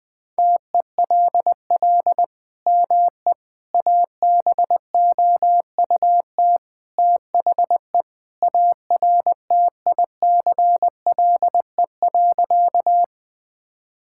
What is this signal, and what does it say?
Morse code, heard as a keyed tone, about 20 wpm, 700 Hz. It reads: TELL ME ABOUT THE ARTICLE.
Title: Sentences from Top 1000 Words - Review Part 5 - 20wpm